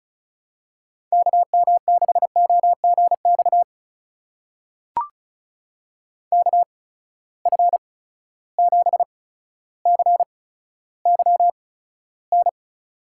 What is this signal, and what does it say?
Morse code, heard as a keyed tone, about 34 wpm, spaced as for 15 wpm, 700 Hz. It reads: KM6OGX KF7CYN
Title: USA Call Signs - 6 Characters - 8x Character Spacing - 35wpm